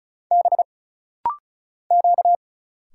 Morse code, heard as a keyed tone, about 34 wpm, 700 Hz. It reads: B Q
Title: Single Letters - Rapid-Fire 35wpm